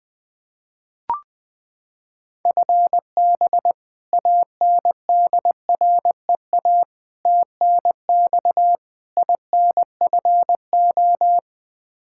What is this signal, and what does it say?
Morse code, heard as a keyed tone, about 20 wpm, 700 Hz. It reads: FB ANDREA TNX INFO
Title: Practice QSO - Exchange 3 - More Info - 20wpm